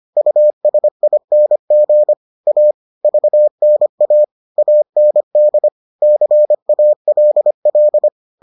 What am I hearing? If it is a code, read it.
Morse code, heard as a keyed tone, about 25 wpm, 600 Hz. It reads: USING A VNA AND CALL